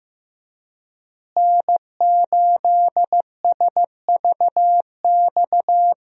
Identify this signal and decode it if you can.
Morse code, heard as a keyed tone, about 15 wpm, 700 Hz. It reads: N8SVX
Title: USA Call Signs - 5 Characters - Repeated 2x - 15wpm